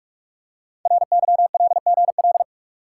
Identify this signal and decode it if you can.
Morse code, heard as a keyed tone, about 44 wpm, 690 Hz. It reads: RYLGL
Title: Code Groups - 5 Letters - 45wpm